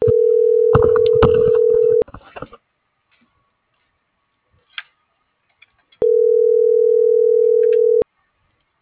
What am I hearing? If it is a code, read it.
Ringback tone (North America), 440 + 480 Hz.